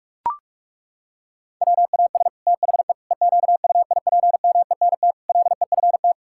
Morse code, heard as a keyed tone, about 41 wpm, 700 Hz. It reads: WAS THE EQUIPMENT LEFT
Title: Sentences from Top 1000 Words - Review Part 5 - 45wpm